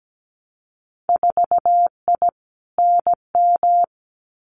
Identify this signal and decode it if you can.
Morse code, heard as a keyed tone, about 17 wpm, 700 Hz. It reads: 4I NM
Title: Winter Field Day Exchanges Encore 17wpm